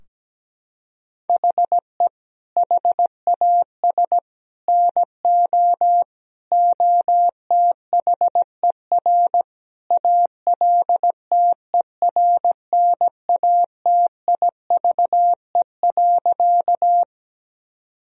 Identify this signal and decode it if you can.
Morse code, heard as a keyed tone, about 17 wpm, 700 Hz. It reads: HE HAS NO OTHER ALTERNATIVE.
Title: Sentences from Top 1000 Words - Review Part 4 - 17wpm